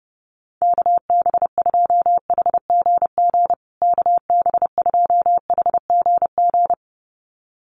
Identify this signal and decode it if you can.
Morse code, heard as a keyed tone, about 30 wpm, 700 Hz. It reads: KB2HGG KB2HGG
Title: USA Call Signs Repeated 2x 30wpm